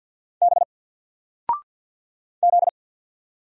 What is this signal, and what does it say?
Morse code, heard as a keyed tone, about 50 wpm, 700 Hz. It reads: B Z
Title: Single Letter-Number 50wpm